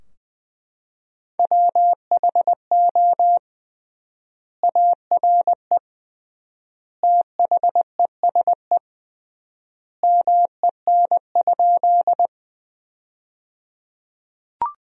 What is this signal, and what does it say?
Morse code, heard as a keyed tone, about 20 wpm, 700 Hz. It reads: WHO ARE THESE MEN?
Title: 3x Word-Spacing - Sentences from Top 200 Words - 20wpm